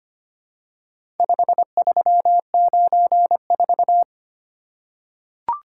Morse code, heard as a keyed tone, about 25 wpm, 700 Hz. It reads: 5394
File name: practice / 25wpm / Four Letter-Number Combinations - 25wpm